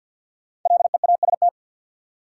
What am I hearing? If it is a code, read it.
Morse code, heard as a keyed tone, about 53 wpm, 700 Hz. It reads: LEAST
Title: Five Letter Words - 50wpm